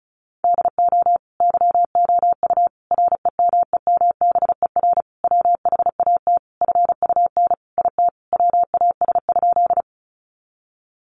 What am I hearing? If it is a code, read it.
Morse code, heard as a keyed tone, about 34 wpm, 700 Hz. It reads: DO YOU REMEMBER WHAT FUN IT WAS?
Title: Sentences from Top 700 Words 35wpm